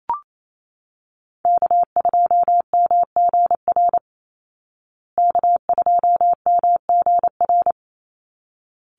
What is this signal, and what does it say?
Morse code, heard as a keyed tone, about 28 wpm, 700 Hz. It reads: K2MGR K2MGR
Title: Call Signs Repeated 3x - 28wpm